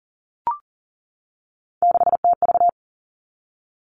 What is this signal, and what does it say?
Morse code, heard as a keyed tone, about 40 wpm, 700 Hz. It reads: 6TV
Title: All Cut Numbers - 3-Digit 40wpm